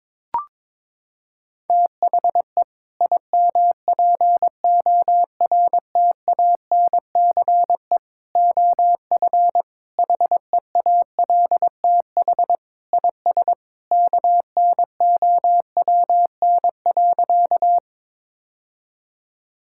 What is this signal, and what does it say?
Morse code, heard as a keyed tone, about 22 wpm, 700 Hz. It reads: THE IMPORTANCE OF HEALTH IS KNOWN.
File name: practice / 22wpm / Sentences with 3-Word Phrases - 22wpm